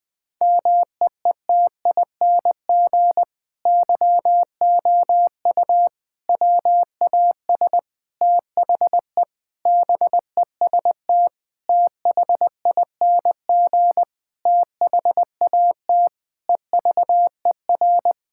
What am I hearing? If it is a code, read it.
Morse code, heard as a keyed tone, about 21 wpm, 700 Hz. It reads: MEETING YOU WAS THE BEST THING THAT EVER
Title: Sentences with 8-Word Phrases - 20wpm